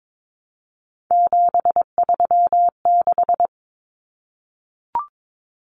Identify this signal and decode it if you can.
Morse code, heard as a keyed tone, about 22 wpm, 700 Hz. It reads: 736
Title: Common Cut Numbers - 3-Digit 22wpm